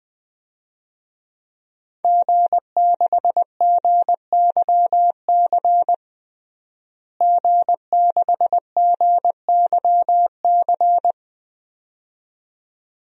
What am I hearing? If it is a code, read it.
Morse code, heard as a keyed tone, about 20 wpm, 700 Hz. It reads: G6GYC G6GYC
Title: DXCC-100 Call Signs - 5 Characters - Repeated 2x - 20wpm